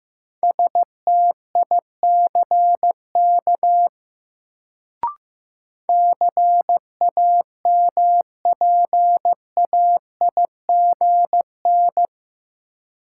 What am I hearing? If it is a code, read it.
Morse code, heard as a keyed tone, about 16 wpm, 700 Hz. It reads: STICK CAMPAIGN
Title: Sentences from Top 1200 Words - New Word Drill Alt - 15wpm